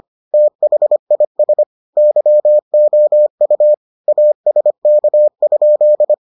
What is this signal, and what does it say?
Morse code, heard as a keyed tone, about 25 wpm, 600 Hz. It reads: THIS YOU ASK?